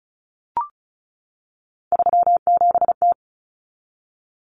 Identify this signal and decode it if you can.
Morse code, heard as a keyed tone, about 34 wpm, 700 Hz. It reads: 37T
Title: Common Cut Numbers - 3-Digit 35wpm